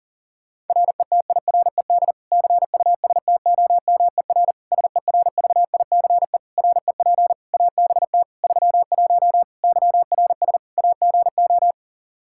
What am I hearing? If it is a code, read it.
Morse code, heard as a keyed tone, about 40 wpm, 700 Hz. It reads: RETIRED CUSTOMER SERVICE REP ABT 31 YRS AGO
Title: Practice QSO - Exchange 3 - More Info - 40wpm